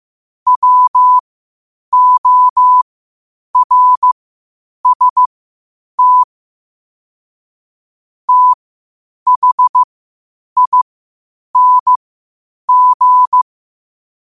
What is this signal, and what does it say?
Morse code, heard as a keyed tone, about 15 wpm, spaced as for 8 wpm, 1000 Hz. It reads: WORST THING
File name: 8wpm